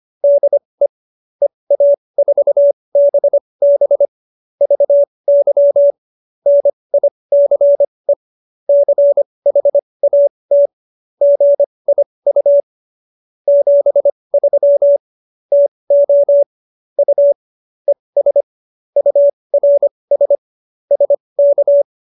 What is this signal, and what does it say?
Morse code, heard as a keyed tone, about 25 wpm, spaced as for 20 wpm, 570 Hz. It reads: DE EA4BB VY NICE CHAT GIU 73 TO U ES URS SK
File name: qso215b 20